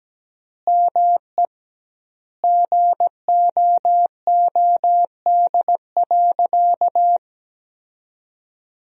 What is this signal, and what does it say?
Morse code, heard as a keyed tone, about 17 wpm, 700 Hz. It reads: ME GOOD.
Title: 2x Word-Spacing - Sentences from Top 500 Words - 17wpm